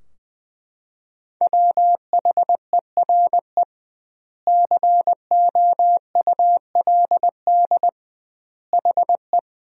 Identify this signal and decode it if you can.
Morse code, heard as a keyed tone, about 20 wpm, 700 Hz. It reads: WHERE COULD HE
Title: 2x Word-Spacing - Sentences from Top 200 Words - 20wpm